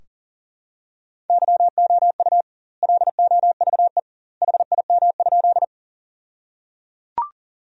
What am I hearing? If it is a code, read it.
Morse code, heard as a keyed tone, about 37 wpm, 700 Hz. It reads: YOU LOVE HIM?
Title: 2x Word-Spacing - Sentences from Top 400 Words - 40wpm